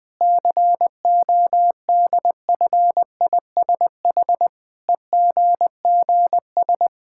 Morse code, heard as a keyed tone, about 20 wpm, 700 Hz. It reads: CODFISH EGGS
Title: Pangram Challenge v2 - 20wpm